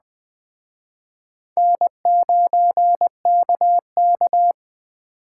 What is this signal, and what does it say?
Morse code, heard as a keyed tone, about 24 wpm, 700 Hz. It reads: N9KK
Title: Speed Racing - USA 1x2 Call Signs 20wpm